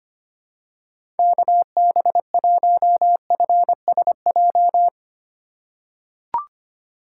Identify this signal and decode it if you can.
Morse code, heard as a keyed tone, about 25 wpm, 700 Hz. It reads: KB1FSJ
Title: USA Call Signs - 2x Character Spacing - 25wpm